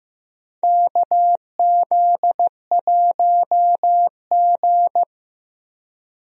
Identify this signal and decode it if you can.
Morse code, heard as a keyed tone, about 15 wpm, 700 Hz. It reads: KZ1G
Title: USA Call Signs - 4 Characters - Repeated 3x - 15wpm